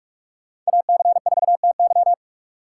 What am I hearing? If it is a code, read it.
Morse code, heard as a keyed tone, about 45 wpm, 690 Hz. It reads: AC4TY
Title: USA Call Signs - 6x Character Spacing - 45wpm